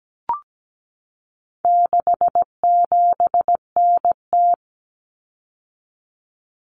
Morse code, heard as a keyed tone, about 17 wpm, 700 Hz. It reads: 67NT
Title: Common Cut Numbers - 4-Digit 17wpm